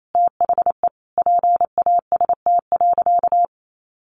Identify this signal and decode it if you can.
Morse code, heard as a keyed tone, about 28 wpm, 700 Hz. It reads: THE PAST.
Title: A Sound Of Thunder 28wpm